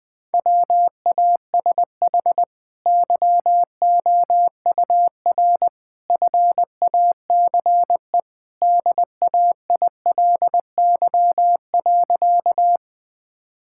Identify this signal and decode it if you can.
Morse code, heard as a keyed tone, about 21 wpm, 700 Hz. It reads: WASH YOUR FACE DAILY.
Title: Easy Sentences with Hints - 20wpm